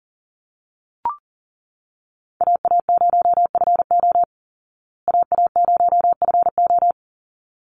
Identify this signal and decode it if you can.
Morse code, heard as a keyed tone, about 40 wpm, 700 Hz. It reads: AA0FO AA0FO
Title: Call Signs Repeated 3x - 40wpm